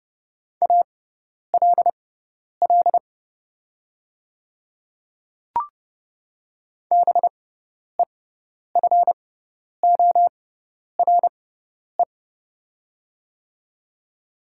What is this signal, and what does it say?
Morse code, heard as a keyed tone, about 30 wpm, spaced as for 10 wpm, 700 Hz. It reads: ALL BEFORE
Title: 6x Character Spacing - Top 200 Words - 30wpm